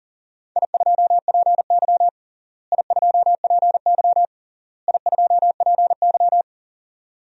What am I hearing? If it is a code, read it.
Morse code, heard as a keyed tone, about 37 wpm, 700 Hz. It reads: I2PY I2PY I2PY
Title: DXCC-100 Call Signs - 4 Characters - Repeated 3x - 40wpm